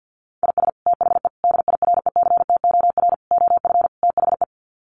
Morse code, heard as a keyed tone, about 50 wpm, 700 Hz. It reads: IS THE DIRECTOR OF THE
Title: Sets of 5 Words Encore2 50wpm